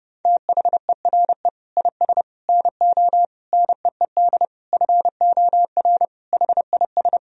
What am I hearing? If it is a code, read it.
Morse code, heard as a keyed tone, about 30 wpm, 700 Hz. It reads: THERE IS NO NEED FOR HIS
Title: Sentences from Top 1000 Words - Review Part 3 - 30wpm